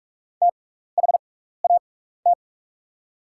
Morse code, heard as a keyed tone, about 44 wpm, spaced as for 15 wpm, 690 Hz. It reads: THAT
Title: 6x Character Spacing - Top 500 Words - 45wpm